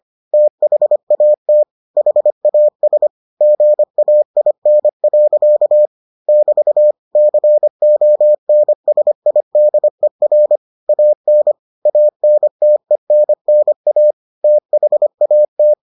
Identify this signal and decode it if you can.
Morse code, heard as a keyed tone, about 25 wpm, 600 Hz. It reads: THAT HAS GAIN. <BT> CONSIDER AN ANTENNA THAT